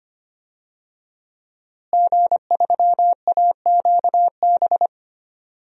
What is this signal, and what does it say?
Morse code, heard as a keyed tone, about 25 wpm, 700 Hz. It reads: G3AQB
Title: DXCC-100 Call Signs - 5 Characters - Repeated 2x - 25wpm